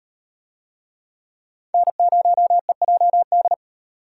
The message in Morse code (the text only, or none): N0EJD